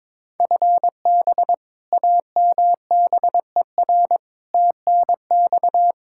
Morse code, heard as a keyed tone, about 22 wpm, 700 Hz. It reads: FB AMBER TNX
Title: Practice QSO - Exchange 3 - More Info - 22wpm